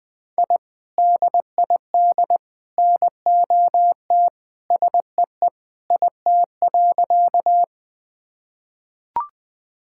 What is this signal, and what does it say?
Morse code, heard as a keyed tone, about 21 wpm, 700 Hz. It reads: I DID NOT SEE IT.
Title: Sentences with 3-Word Phrases - Spoken Beginning - 20wpm